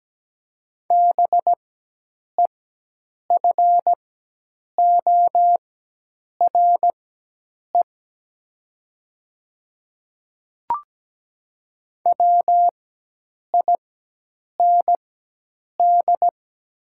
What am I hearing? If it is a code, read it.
Morse code, heard as a keyed tone, about 17 wpm, spaced as for 8 wpm, 700 Hz. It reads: BEFORE WIND
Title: 4x Character Spacing - Top 500 Words - 17wpm